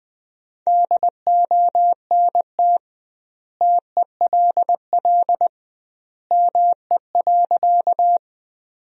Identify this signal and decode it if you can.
Morse code, heard as a keyed tone, about 20 wpm, 700 Hz. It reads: DONT TELL ME.
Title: 2x Word-Spacing - Sentences from Top 300 Words - 20wpm